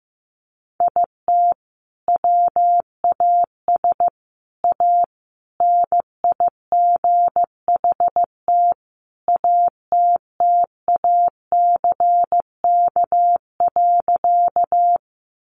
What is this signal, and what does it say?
Morse code, heard as a keyed tone, about 15 wpm, 700 Hz. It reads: IT WAS A NIGHT ATTACK.
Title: Sentences from Top 1500 Words 15wpm